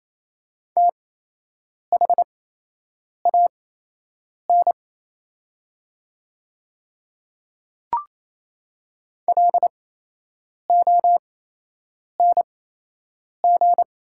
Morse code, heard as a keyed tone, about 28 wpm, spaced as for 8 wpm, 700 Hz. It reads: THAN LONG
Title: 8x Character Spacing - Top 200 Words - 28wpm